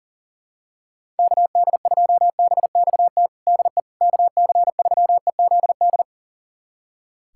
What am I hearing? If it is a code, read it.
Morse code, heard as a keyed tone, about 40 wpm, 700 Hz. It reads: KD2BXT DE KC3EZD
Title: USA Callsign DE Callsign 40wpm